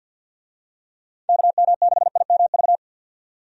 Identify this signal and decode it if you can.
Morse code, heard as a keyed tone, about 53 wpm, 700 Hz. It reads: KM6IMV
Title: USA Call Signs - 6 Characters - 50wpm